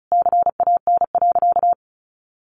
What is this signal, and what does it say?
Morse code, heard as a keyed tone, about 34 wpm, 700 Hz. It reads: CAN.
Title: Additional Easy Phrases 35wpm